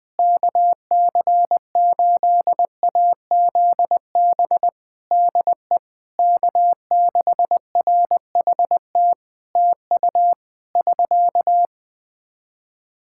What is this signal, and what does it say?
Morse code, heard as a keyed tone, about 20 wpm, 700 Hz. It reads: KC8AZB DE K6RHT TU <SK>
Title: Practice QSO - Exchange 4 - Ending - 20wpm